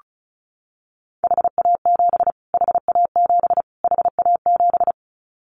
Morse code, heard as a keyed tone, about 34 wpm, 700 Hz. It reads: HA7 HA7 HA7
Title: Callsign Prefixes for Top 25 DXCC - Repeated 3x - 35wpm